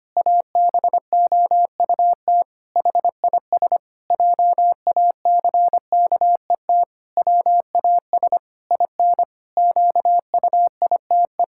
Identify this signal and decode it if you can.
Morse code, heard as a keyed tone, about 25 wpm, 700 Hz. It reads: ABOUT HIS JACKET WAS IN QUITE
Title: Pangram Challenge v2 - 25wpm